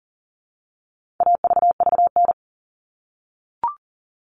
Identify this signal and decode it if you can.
Morse code, heard as a keyed tone, about 39 wpm, 700 Hz. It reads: AVVN
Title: All Cut Numbers - 4-Digit 40wpm